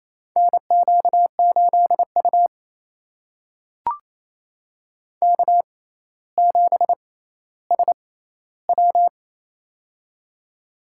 Morse code, heard as a keyed tone, about 28 wpm, spaced as for 14 wpm, 700 Hz. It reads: NQ8U K7SW
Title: USA Call Signs - 4 Characters - 6x Character Spacing - 28wpm